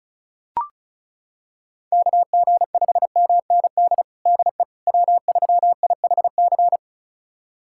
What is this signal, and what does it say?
Morse code, heard as a keyed tone, about 36 wpm, 700 Hz. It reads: KG5MND DE W3IHC
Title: USA Callsign DE Callsign 35wpm